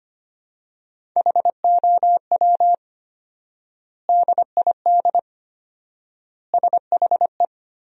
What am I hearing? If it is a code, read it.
Morse code, heard as a keyed tone, about 25 wpm, 700 Hz. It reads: HOW DID SHE
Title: 4x Word-Spacing - Sentences from Top 100 Words - 25wpm